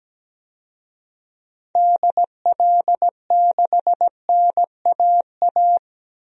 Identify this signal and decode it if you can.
Morse code, heard as a keyed tone, about 17 wpm, 700 Hz. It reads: DL6NAA
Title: DXCC-100 Call Signs - 6 Characters - Repeated 3x - 17wpm